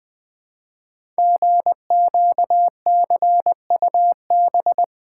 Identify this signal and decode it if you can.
Morse code, heard as a keyed tone, about 20 wpm, 700 Hz. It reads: GQCUB
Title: Code Groups - 5 Letter:Numbers - 20wpm